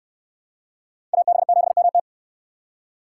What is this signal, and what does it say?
Morse code, heard as a keyed tone, about 62 wpm, 700 Hz. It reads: AB7MT